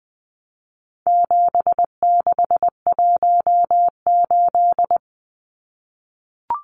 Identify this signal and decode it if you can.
Morse code, heard as a keyed tone, about 20 wpm, 700 Hz. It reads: 7618
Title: Numbers - 4-Digit 20wpm